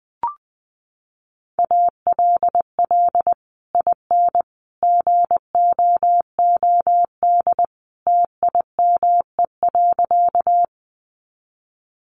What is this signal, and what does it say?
Morse code, heard as a keyed tone, about 20 wpm, 700 Hz. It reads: ALL IN GOOD TIME.
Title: Sentences from Top 200 Words 20wpm